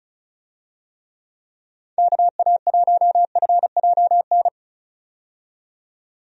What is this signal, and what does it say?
Morse code, heard as a keyed tone, about 34 wpm, 700 Hz. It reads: KA1FJN
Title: USA Call Signs - 6 Characters - 35wpm